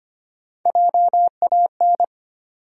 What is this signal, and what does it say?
Morse code, heard as a keyed tone, about 25 wpm, 700 Hz. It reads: JAN
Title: USA Common Names - Max 5 Letters - 25wpm